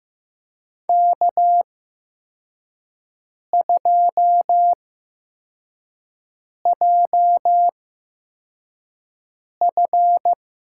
Morse code, heard as a keyed tone, about 15 wpm, spaced as for 4 wpm, 700 Hz. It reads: K2JF
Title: USA Call Signs - 5 Characters - 8x Character Spacing - 15wpm